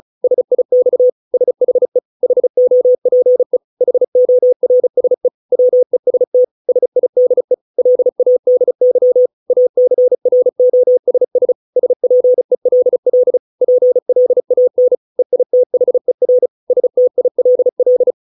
Morse code, heard as a keyed tone, about 34 wpm, 500 Hz. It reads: SIX SHE HOPE HORSE WEST SIDE LADY ACROSS SPELL PLAN EITHER STILL